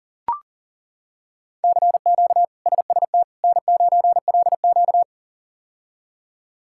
Morse code, heard as a keyed tone, about 38 wpm, 700 Hz. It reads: CQ SST N9LQ
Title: K1USN SST QSO Practice - 40wpm